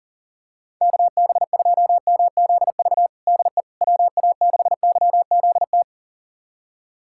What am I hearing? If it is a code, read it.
Morse code, heard as a keyed tone, about 40 wpm, 700 Hz. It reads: KB2MZV DE WA6YZT
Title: USA Callsign DE Callsign - 6 Characters - 40wpm